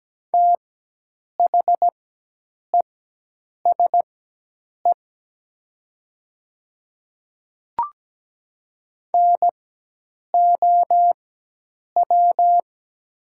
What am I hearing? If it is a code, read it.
Morse code, heard as a keyed tone, about 17 wpm, spaced as for 8 wpm, 700 Hz. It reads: THESE NOW
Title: 4x Character Spacing - Top 100 Words - 17wpm